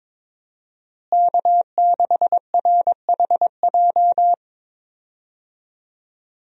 Morse code, heard as a keyed tone, about 22 wpm, 700 Hz. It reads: K6RHJ
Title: USA Call Signs - 5 Characters - Repeated 2x - 22wpm